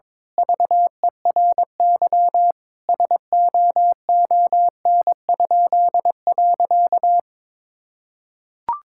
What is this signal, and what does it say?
Morse code, heard as a keyed tone, about 22 wpm, 700 Hz. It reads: VERY SOON?.
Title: Pangram Challenge v2 - 22wpm